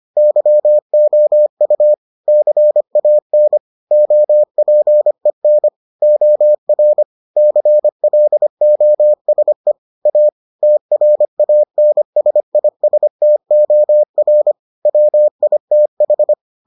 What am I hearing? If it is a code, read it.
Morse code, heard as a keyed tone, about 25 wpm, 600 Hz. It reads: YOU CAN OPEN OR CLOSE A TRANSISTOR WITH